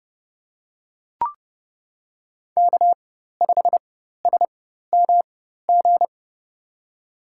Morse code, heard as a keyed tone, about 29 wpm, spaced as for 14 wpm, 700 Hz. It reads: K5SMG
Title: USA Call Signs - 5 Characters - 4x Character Spacing - 30wpm